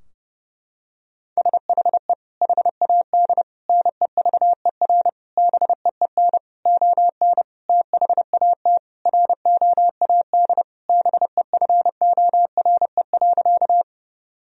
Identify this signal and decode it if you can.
Morse code, heard as a keyed tone, about 30 wpm, 700 Hz. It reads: SHE HAD NEVER BEEN ON THAT ROAD BEFORE.
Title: Sentences from Top 1000 Words - Review Part 2 - 30wpm